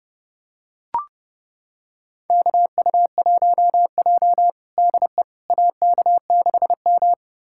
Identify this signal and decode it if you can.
Morse code, heard as a keyed tone, about 30 wpm, 700 Hz. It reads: KU1J DE AK6M
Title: USA Callsign DE Callsign - 4 Characters - 30wpm